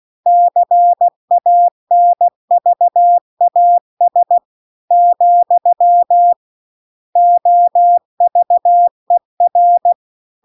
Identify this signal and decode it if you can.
Morse code, heard as a keyed tone, about 16 wpm, 700 Hz. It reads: CANVAS , OVER